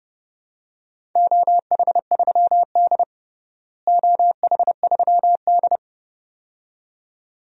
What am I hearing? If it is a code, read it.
Morse code, heard as a keyed tone, about 30 wpm, 700 Hz. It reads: OH3D OH3D
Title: DXCC-100 Call Signs - 4 Characters - Repeated 2x - 30wpm